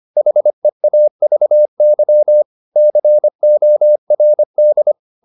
Morse code, heard as a keyed tone, about 25 wpm, 600 Hz. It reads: HEAVY CORD
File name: the old man and the sea part 1 25wpm